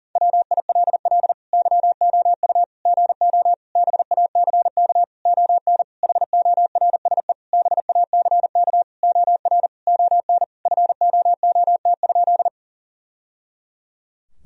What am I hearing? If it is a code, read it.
Morse code, heard as a keyed tone, about 40 wpm, 700 Hz. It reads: WILL YOU GO BACK ON HORSE BACK OR ON FOOT?
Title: Sentences from Top 1000 Words - Review Part 3 - 40wpm